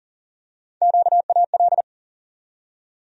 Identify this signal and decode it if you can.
Morse code, heard as a keyed tone, about 38 wpm, 700 Hz. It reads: QAL
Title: USA Call Sign Postfixes - 3 Letters 40wpm